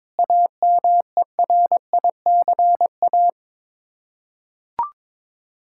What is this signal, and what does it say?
Morse code, heard as a keyed tone, about 22 wpm, 700 Hz. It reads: AMERICA
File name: practice / 22wpm / Sets of 8 Words - 22wpm